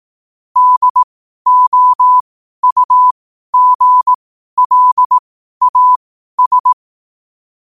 Morse code, heard as a keyed tone, about 18 wpm, spaced as for 12 wpm, 1000 Hz. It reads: DOUGLAS
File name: qotdfiles / 12wpm